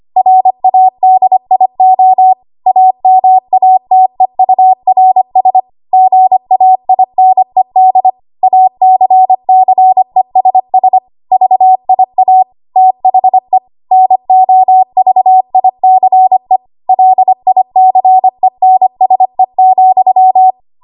Morse code, heard as a keyed tone, about 25 wpm, 750 Hz. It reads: RADIO AMATEURS GAINED ACCESS VIA THE NOVICE LICENSE,
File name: Archive / 25 WPM / 25WPM